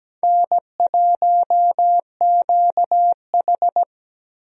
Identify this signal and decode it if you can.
Morse code, heard as a keyed tone, about 17 wpm, 700 Hz. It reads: N1QH
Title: USA Call Signs - 4 Characters - Repeated 3x - 17wpm